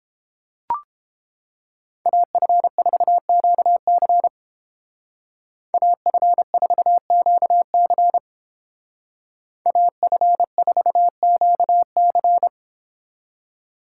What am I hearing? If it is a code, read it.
Morse code, heard as a keyed tone, about 29 wpm, 700 Hz. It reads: AF4QC AF4QC AF4QC